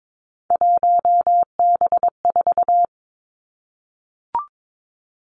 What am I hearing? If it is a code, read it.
Morse code, heard as a keyed tone, about 22 wpm, 700 Hz. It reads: 1B4
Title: Three Challenging Letter-Number Combinations 22wpm